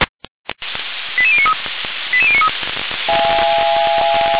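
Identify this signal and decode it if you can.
Five-tone selective call (ZVEI), heard as quick runs of five tones, tones of about 70 ms. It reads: 88093 88093
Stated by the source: ZVEI five-tone selective call received by SDR.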